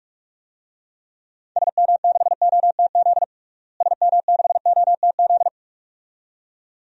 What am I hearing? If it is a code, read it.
Morse code, heard as a keyed tone, about 46 wpm, 700 Hz. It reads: SM6OTZ SM6OTZ
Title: DXCC-100 Call Signs - 6 Characters - Repeated 2x - 45wpm